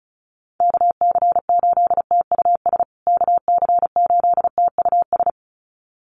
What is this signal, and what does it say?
Morse code, heard as a keyed tone, about 34 wpm, 700 Hz. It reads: KC8TUS KC8TUS
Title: USA Call Signs Repeated 2x 35wpm